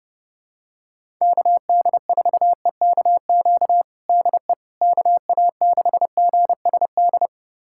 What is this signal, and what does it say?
Morse code, heard as a keyed tone, about 30 wpm, 700 Hz. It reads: KD4EKQ DE KA6GSD
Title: USA Callsign DE Callsign - 6 Characters - 30wpm